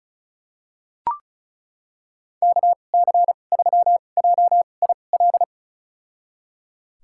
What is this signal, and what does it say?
Morse code, heard as a keyed tone, about 34 wpm, spaced as for 25 wpm, 700 Hz. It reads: KC3JIL
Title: USA Call Signs - 6 Characters - 2x Character Spacing - 35wpm